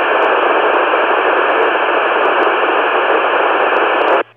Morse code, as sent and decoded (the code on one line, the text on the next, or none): none